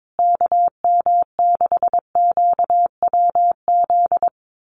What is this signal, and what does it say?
Morse code, heard as a keyed tone, about 22 wpm, 700 Hz. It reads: KM6QWZ
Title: USA Call Signs Repeated 2x 22wpm